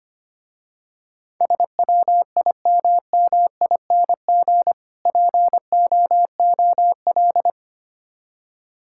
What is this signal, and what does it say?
Morse code, heard as a keyed tone, about 25 wpm, 700 Hz. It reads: SWIMMING POOL
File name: Easy Words - Speed-Racing with Hints - 25wpm